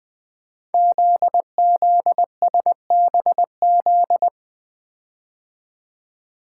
Morse code, heard as a keyed tone, about 20 wpm, 700 Hz. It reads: ZZSBZ
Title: Code Groups - 5 Letters - 20wpm